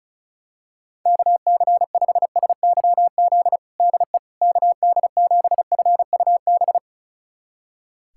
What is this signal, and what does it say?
Morse code, heard as a keyed tone, about 36 wpm, 700 Hz. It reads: KC5SYZ DE KD7FUB
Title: USA Callsign DE Callsign - 6 Characters - 35wpm